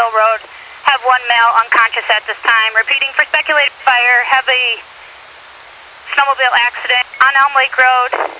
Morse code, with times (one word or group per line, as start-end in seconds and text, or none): none